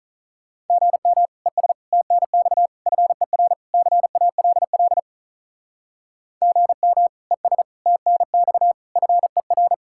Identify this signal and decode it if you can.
Morse code, heard as a keyed tone, about 46 wpm, 700 Hz. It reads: GM ES TNX FER CALL GM ES TNX FER